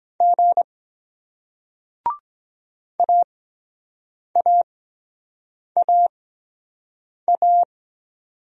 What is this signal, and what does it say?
Morse code, heard as a keyed tone, about 23 wpm, spaced as for 7 wpm, 700 Hz. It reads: G AAAA